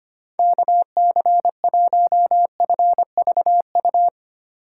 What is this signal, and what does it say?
Morse code, heard as a keyed tone, about 31 wpm, 700 Hz. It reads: KC1FVU
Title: Speed Racing - USA 2x3 Call Signs 25wpm